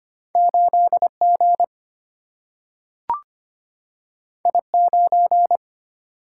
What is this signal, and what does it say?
Morse code, heard as a keyed tone, about 25 wpm, 700 Hz. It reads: 8G I9
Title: Single-Letter Single-Number 25wpm